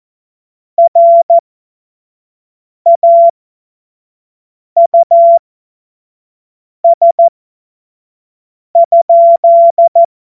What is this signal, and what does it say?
Morse code, heard as a keyed tone, about 14 wpm, spaced as for 5 wpm, 680 Hz. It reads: RAUS?
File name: Morsekurs / Witze / Witz 02 April 2024 25 F70